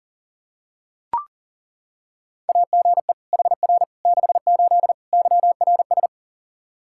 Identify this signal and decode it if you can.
Morse code, heard as a keyed tone, about 37 wpm, 700 Hz. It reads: AGE HR 68 YRS
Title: Practice QSO - Exchange 3 - More Info - 40wpm